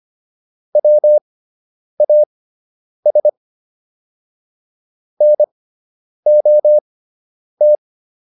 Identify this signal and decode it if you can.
Morse code, heard as a keyed tone, about 25 wpm, spaced as for 9 wpm, 600 Hz. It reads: WAS NOT